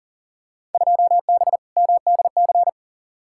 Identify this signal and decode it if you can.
Morse code, heard as a keyed tone, about 40 wpm, 700 Hz. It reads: 2B MDC
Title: Field Day Exchanges - 40wpm